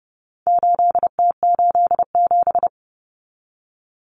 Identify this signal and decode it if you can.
Morse code, heard as a keyed tone, about 30 wpm, 700 Hz. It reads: 8T87
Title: Common Cut Numbers - 4-Digit 30wpm